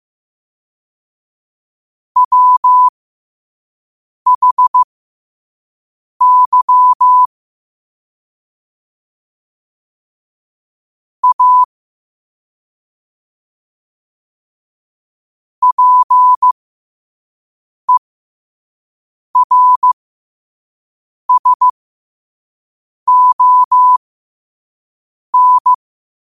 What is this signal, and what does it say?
Morse code, heard as a keyed tone, about 15 wpm, spaced as for 5 wpm, 1000 Hz. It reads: WHY A PERSON
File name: qotdfiles / 5wpm